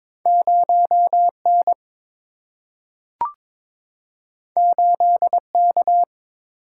Morse code, heard as a keyed tone, about 21 wpm, 700 Hz. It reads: 0 N 8 K
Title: Single-Letter Single-Number 22wpm